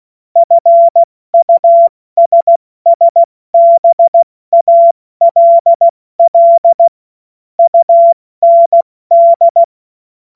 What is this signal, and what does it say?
Morse code, heard as a keyed tone, about 16 wpm, 680 Hz. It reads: FUSSBALL UND
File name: Morsekurs / Witze / Witz 10. August 2021, 70 F80